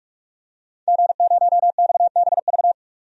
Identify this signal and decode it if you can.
Morse code, heard as a keyed tone, about 42 wpm, 700 Hz. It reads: G0XBV
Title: DXCC-100 Call Signs - 5 Characters - 45wpm